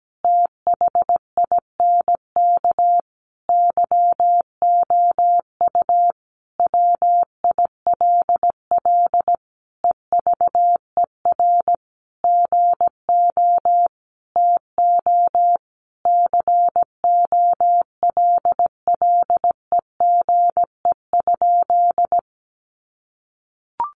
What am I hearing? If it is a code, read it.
Morse code, heard as a keyed tone, about 17 wpm, 700 Hz. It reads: THINK YOU WILL EVER GO TO COLLEGE?
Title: Sentences from Top 1300 Words 17wpm